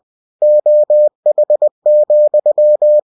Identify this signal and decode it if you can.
Morse code, heard as a keyed tone, about 20 wpm, 600 Hz. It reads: OH,
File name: winnie-20wpm